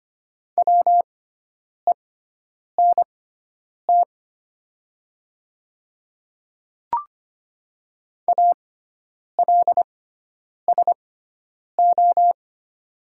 Morse code, heard as a keyed tone, about 25 wpm, spaced as for 9 wpm, 700 Hz. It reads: WENT ALSO
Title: 6x Character Spacing - Top 200 Words - 25wpm